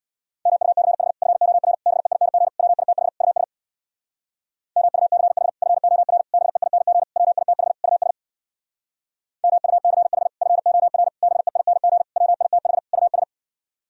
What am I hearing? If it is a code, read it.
Morse code, heard as a keyed tone, about 65 wpm, 700 Hz. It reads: MUCH FOR BEING WITH US MUCH FOR BEING WITH US MUCH FOR BEING WITH US